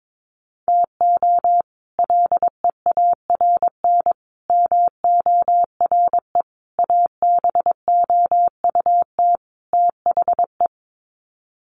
Morse code, heard as a keyed tone, about 22 wpm, 700 Hz. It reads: TO LEARN MORE ABOUT THE
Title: Sets of 5 Words Encore2 22wpm